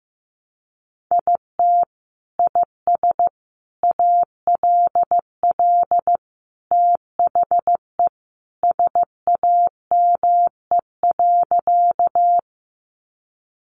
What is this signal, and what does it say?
Morse code, heard as a keyed tone, about 15 wpm, 700 Hz. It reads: IT IS ALL THE SAME.
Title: Sentences from Top 200 Words 15wpm